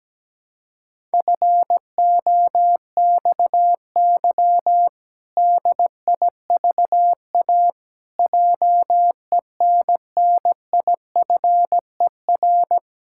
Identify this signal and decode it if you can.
Morse code, heard as a keyed tone, about 17 wpm, 700 Hz. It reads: FOXY DIVA JENNIFER
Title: Pangram Challenge v2 - 17wpm